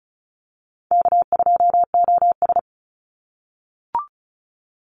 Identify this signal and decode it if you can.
Morse code, heard as a keyed tone, about 34 wpm, 700 Hz. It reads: K2OS
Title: Call Signs - 35wpm